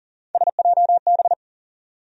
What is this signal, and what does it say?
Morse code, heard as a keyed tone, about 40 wpm, 700 Hz. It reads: SJB
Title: USA Call Sign Postfixes - 3 Letters 40wpm